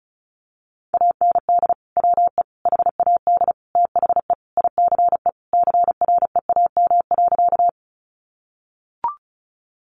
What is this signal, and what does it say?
Morse code, heard as a keyed tone, about 34 wpm, 700 Hz. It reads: AND WE HAD THE ICE CREAM.
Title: Sentences from Top 1500 Words 35wpm